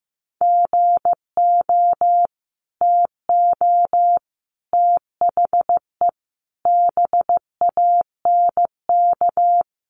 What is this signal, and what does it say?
Morse code, heard as a keyed tone, about 15 wpm, 700 Hz. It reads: GO TO THE BANK
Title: Sentences from Top 1500 Words 15wpm